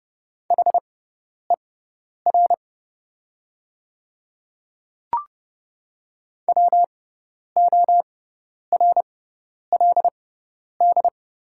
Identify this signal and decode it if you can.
Morse code, heard as a keyed tone, about 30 wpm, spaced as for 10 wpm, 690 Hz. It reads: HER WORLD
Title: 6x Character Spacing - Top 300 Words - 30wpm